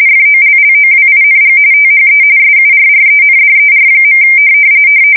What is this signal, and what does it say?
RTTY, about 45.45 baud. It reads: FOLLOWING THE SURRENDER.
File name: fsk170 / USS-Missouri